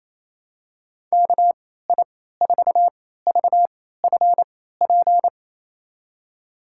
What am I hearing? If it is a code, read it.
Morse code, heard as a keyed tone, about 28 wpm, spaced as for 16 wpm, 700 Hz. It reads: KI4VFP
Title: USA Call Signs - 6 Characters - 3x Character Spacing - 28wpm